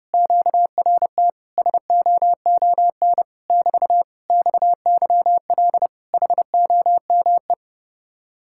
Morse code, heard as a keyed tone, about 32 wpm, 700 Hz. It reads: QRT SOON <BT> XYL HOME